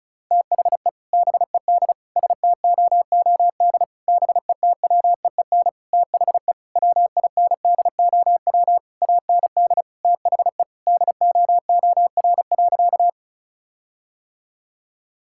Morse code, heard as a keyed tone, about 36 wpm, 700 Hz. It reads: THE BED STOOD BETWEEN THE WINDOW AND THE DOOR.
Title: Sentences from Top 1600 Words 35wpm